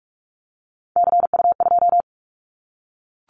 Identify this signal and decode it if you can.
Morse code, heard as a keyed tone, about 44 wpm, 700 Hz. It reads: CU2
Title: Callsign Prefixes for Top 100 DXCC - 45wpm